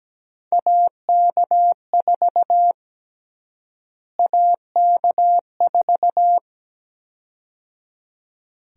Morse code, heard as a keyed tone, about 17 wpm, 700 Hz. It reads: AK4 AK4
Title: USA Call Sign Prefixes - Repeated 2x - 17wpm